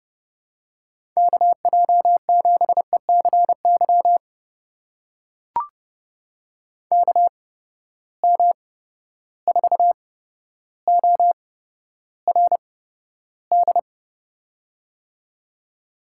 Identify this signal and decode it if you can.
Morse code, heard as a keyed tone, about 30 wpm, spaced as for 13 wpm, 700 Hz. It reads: KJ7ECY KM4ORD
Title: USA Call Signs - 6 Characters - 8x Character Spacing - 30wpm